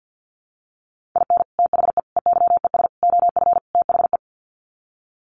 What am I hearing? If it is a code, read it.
Morse code, heard as a keyed tone, about 50 wpm, 700 Hz. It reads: IN THE EYES OF THE
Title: Sets of 5 Words 50wpm